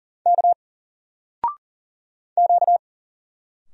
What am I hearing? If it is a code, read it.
Morse code, heard as a keyed tone, about 40 wpm, 700 Hz. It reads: K Q
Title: Single Character (Letter:Number:Punctuation:Prosign) 40wpm